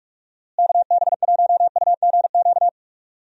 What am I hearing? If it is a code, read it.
Morse code, heard as a keyed tone, about 46 wpm, 700 Hz. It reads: KB1UGQ
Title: USA Call Signs - 6 Characters - Repeated 3x - 45wpm